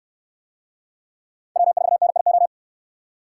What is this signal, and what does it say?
Morse code, heard as a keyed tone, about 66 wpm, 700 Hz. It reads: W4NIO